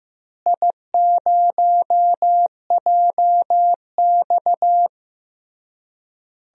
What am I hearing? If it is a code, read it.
Morse code, heard as a keyed tone, about 15 wpm, 700 Hz. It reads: I0JX
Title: DXCC-100 Call Signs - 4 Characters - Repeated 3x - 15wpm